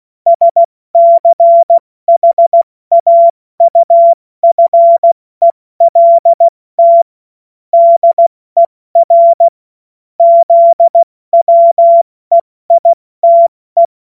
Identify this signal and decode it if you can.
Morse code, heard as a keyed tone, about 16 wpm, 680 Hz. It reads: SCHAUFELT DER ZWEITE
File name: Morsekurs / Witze / Witz 22. August 2023 70 F80